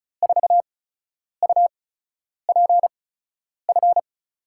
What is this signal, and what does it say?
Morse code, heard as a keyed tone, about 36 wpm, spaced as for 10 wpm, 700 Hz. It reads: 4UPF
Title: USA Call Signs - 6 Characters - 8x Character Spacing - 35wpm